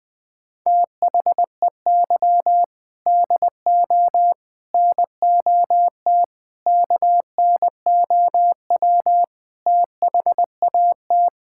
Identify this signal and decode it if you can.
Morse code, heard as a keyed tone, about 20 wpm, 700 Hz. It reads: THEY DO NOT KNOW THAT
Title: Sentences with 4-Word Phrases - Spoken Beginning - 20wpm